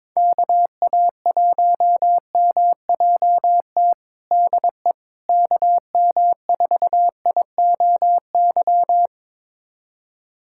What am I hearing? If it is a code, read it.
Morse code, heard as a keyed tone, about 22 wpm, 700 Hz. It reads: KA1MJT DE KM4IOY
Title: USA Callsign DE Callsign - 6 Characters - 22wpm